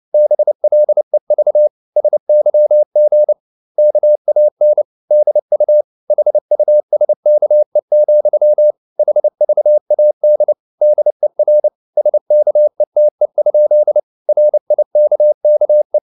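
Morse code, heard as a keyed tone, about 29 wpm, 600 Hz. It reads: BLEV SYG KAN DU HUSKE, HVAD DER SKETE? RIKKE
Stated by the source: Fast (QRQ) Morse at 29 wpm, from CW med Gnister.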